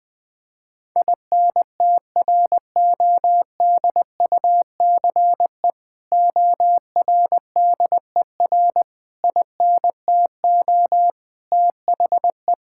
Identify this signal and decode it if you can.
Morse code, heard as a keyed tone, about 21 wpm, 700 Hz. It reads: INTRODUCE ORDER INTO THE
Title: Sentences from Top 1600 Words 20wpm